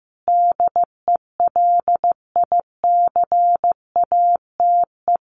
Morse code, heard as a keyed tone, about 15 wpm, 700 Hz. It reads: DELICATE
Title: A Sound Of Thunder 15wpm